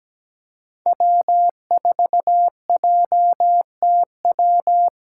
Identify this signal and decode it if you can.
Morse code, heard as a keyed tone, about 17 wpm, 700 Hz. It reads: W4JTW
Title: USA Call Signs - 2x Character Spacing - 17wpm